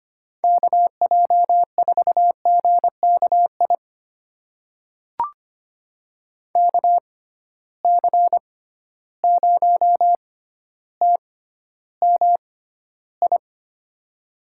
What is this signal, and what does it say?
Morse code, heard as a keyed tone, about 25 wpm, spaced as for 13 wpm, 700 Hz. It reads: KJ4GKI KC0TMI
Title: USA Call Signs - 6 Characters - 6x Character Spacing - 25wpm